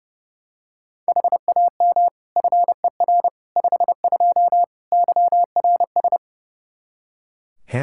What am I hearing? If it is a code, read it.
Morse code, heard as a keyed tone, about 30 wpm, 700 Hz. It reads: HAM FER 52 YRS
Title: Practice QSO - Exchange 3 - More Info - 30wpm